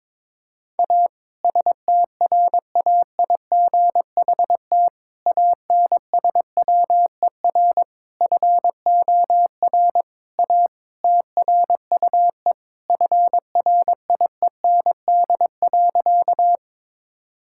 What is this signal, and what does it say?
Morse code, heard as a keyed tone, about 22 wpm, 700 Hz. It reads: A STRAIGHT ANSWER FOR A TRUE FRIEND.
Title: Sentences from Top 1000 Words - Review Part 6 - 22wpm